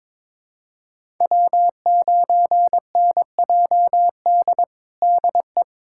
Morse code, heard as a keyed tone, about 22 wpm, 700 Hz. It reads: W9NJD DE
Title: USA Callsign DE Callsign - 5 Characters - 22wpm